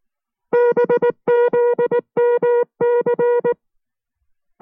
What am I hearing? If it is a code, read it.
Morse code, heard as a keyed tone, about 19 wpm, 480 Hz. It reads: BZMC